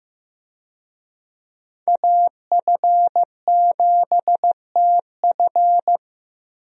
Morse code, heard as a keyed tone, about 15 wpm, 700 Hz. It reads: AF7TF
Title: USA Call Signs - 5 Characters - Repeated 2x - 15wpm